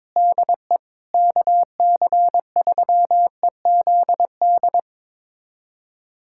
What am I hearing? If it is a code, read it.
Morse code, heard as a keyed tone, about 21 wpm, 700 Hz. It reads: DE KC3EZD
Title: USA Callsign DE Callsign 22wpm